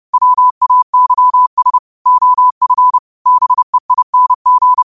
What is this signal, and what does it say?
Morse code, heard as a keyed tone, about 30 wpm, 1000 Hz. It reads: WAYS OF BEING
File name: qotdfiles / 30wpm